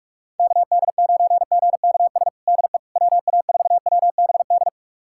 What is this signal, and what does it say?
Morse code, heard as a keyed tone, about 41 wpm, 700 Hz. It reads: KD9GKS DE WA4WBD
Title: USA Callsign DE Callsign - 6 Characters - 45wpm